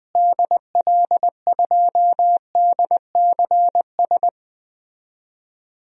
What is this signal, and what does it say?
Morse code, heard as a keyed tone, about 21 wpm, 700 Hz. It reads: DL2DCS
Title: DXCC-100 Call Signs - 6 Characters - Repeated 3x - 20wpm